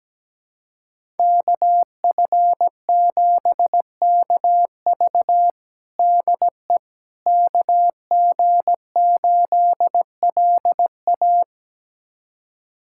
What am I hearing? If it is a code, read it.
Morse code, heard as a keyed tone, about 17 wpm, 700 Hz. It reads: KF7KV DE KG8LA
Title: USA Callsign DE Callsign - 5 Characters - 17wpm